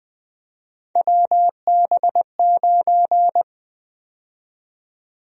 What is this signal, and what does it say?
Morse code, heard as a keyed tone, about 20 wpm, 700 Hz. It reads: WB9
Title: USA Call Sign Prefixes - Repeated 2x - 20wpm